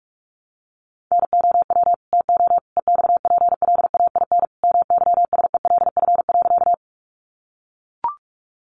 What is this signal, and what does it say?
Morse code, heard as a keyed tone, about 44 wpm, 700 Hz. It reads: NOW TO EXPLAIN MYSELF.
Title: Sentences from Top 1300 Words 45wpm